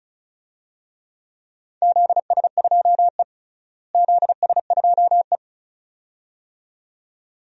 Morse code, heard as a keyed tone, about 34 wpm, 700 Hz. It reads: ZS2E ZS2E
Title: DXCC-100 Call Signs - 4 Characters - Repeated 2x - 35wpm